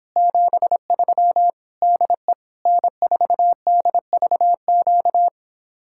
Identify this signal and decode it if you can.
Morse code, heard as a keyed tone, about 26 wpm, 700 Hz. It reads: 73 DE N4DVQ